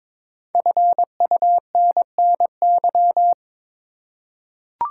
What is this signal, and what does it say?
Morse code, heard as a keyed tone, about 22 wpm, 700 Hz. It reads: FUNNY
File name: 5 Letter Words - First 3 Letters Given - 22wpm